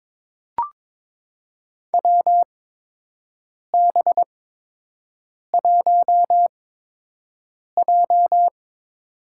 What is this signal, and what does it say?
Morse code, heard as a keyed tone, about 22 wpm, spaced as for 6 wpm, 700 Hz. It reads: WB1J
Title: USA Call Signs - 4 Characters - 8x Character Spacing - 22wpm